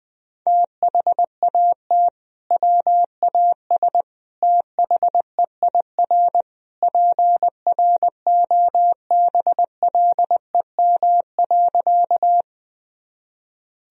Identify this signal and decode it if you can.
Morse code, heard as a keyed tone, about 20 wpm, 700 Hz. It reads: THAT WAS THEIR PROBLEM.
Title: Sentences from Top 1000 Words - Review Part 2 - 20wpm